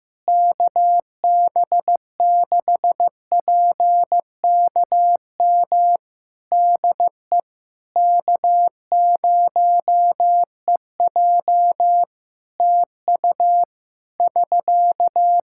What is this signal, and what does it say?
Morse code, heard as a keyed tone, about 15 wpm, 700 Hz. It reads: KB6PKM DE K0EJ TU <SK>
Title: Practice QSO - Exchange 4 - Ending - 15wpm